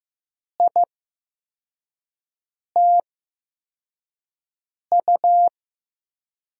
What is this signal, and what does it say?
Morse code, heard as a keyed tone, about 15 wpm, spaced as for 4 wpm, 700 Hz. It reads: ITU
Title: USA Call Signs - 5 Characters - 8x Character Spacing - 15wpm